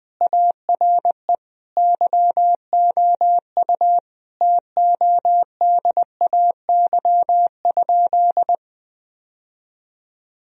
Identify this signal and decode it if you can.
Morse code, heard as a keyed tone, about 21 wpm, 700 Hz. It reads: ARE YOU TODAY?
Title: Sentences from Top 1000 Words - Review Part 3 - 20wpm